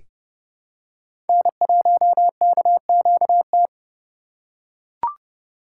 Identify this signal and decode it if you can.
Morse code, heard as a keyed tone, about 30 wpm, 700 Hz. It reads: N1KQT
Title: USA Call Signs - 5 Characters - 8x Character Spacing - 30wpm